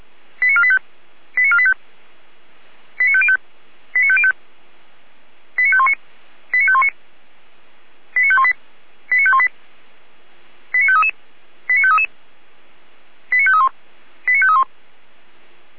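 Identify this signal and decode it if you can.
Five-tone selective call (ZVEI), heard as quick runs of five tones, tones of about 70 ms. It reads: 89485 89485 89505 89505 89519 89519 89518 89518 89533 89533 89531 89531
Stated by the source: ZVEI five-tone selective call received on an SDR.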